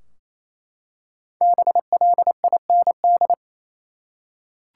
Morse code, heard as a keyed tone, about 28 wpm, 700 Hz. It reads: BLIND
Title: Five Letter Words - 28wpm